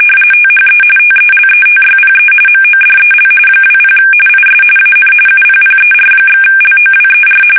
RTTY, about 45.45 baud. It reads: THE TELETYPE MODEL 20 IS AN UPPER/LOWER
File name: fsk850 / teletype-corp